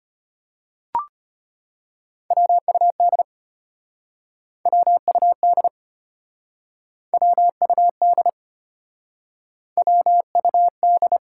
Morse code, heard as a keyed tone, about 34 wpm, 700 Hz. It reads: WUD WUD WUD WUD